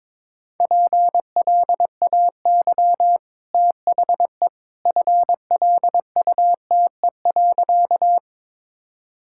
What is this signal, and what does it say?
Morse code, heard as a keyed tone, about 22 wpm, 690 Hz. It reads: PLAY THE FLUTE.
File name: practice / 22wpm / Easy Sentences with Hints - 22wpm